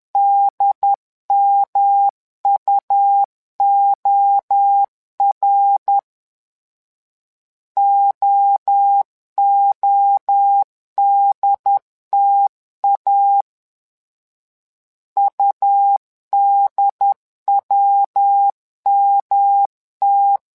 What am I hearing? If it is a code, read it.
Morse code, heard as a keyed tone, about 11 wpm, 800 Hz. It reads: DMUOR OODTA UDWMT
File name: LEZIONI / 3 MWRO / 3 MWRO 10wpm 2148